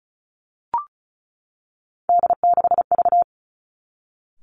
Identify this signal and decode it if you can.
Morse code, heard as a keyed tone, about 36 wpm, 700 Hz. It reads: D6V
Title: Three Challenging Letter-Number Combinations 35wpm